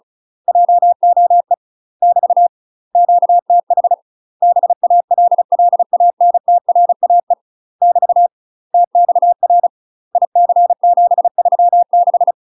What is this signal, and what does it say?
Morse code, heard as a keyed tone, about 36 wpm, 700 Hz. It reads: JOE <BT> QTH BALLANTRAE <BT> TXR IC736
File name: UK MASTER QSOs TRAINING / 35WPM / UK MASTER QSOs 0009